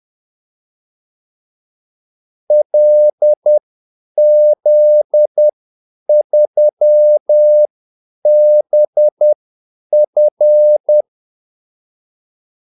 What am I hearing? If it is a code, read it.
Morse code, heard as a keyed tone, about 10 wpm, spaced as for 8 wpm, 600 Hz. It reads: LZ3BF
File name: Callsigns-10-8wpm